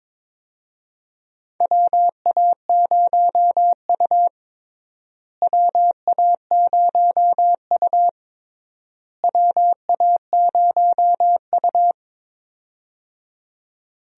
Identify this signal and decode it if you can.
Morse code, heard as a keyed tone, about 21 wpm, 700 Hz. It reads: WA0U WA0U WA0U
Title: USA Call Signs - 4 Characters - Repeated 3x - 22wpm